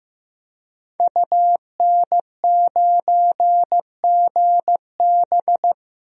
Morse code, heard as a keyed tone, about 15 wpm, 700 Hz. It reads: UN9GB
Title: DXCC-100 Call Signs - 5 Characters - Repeated 2x - 15wpm